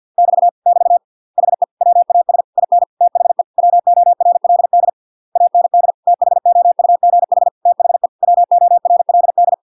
Morse code, heard as a keyed tone, about 50 wpm, 700 Hz. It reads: <BT> <BT> HE WAS IN THE WORLD AND THOUGH THE WORLD